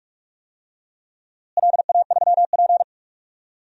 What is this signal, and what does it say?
Morse code, heard as a keyed tone, about 45 wpm, 700 Hz. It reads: LA3P
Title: DXCC-100 Call Signs - 4 Characters - Repeated 2x - 45wpm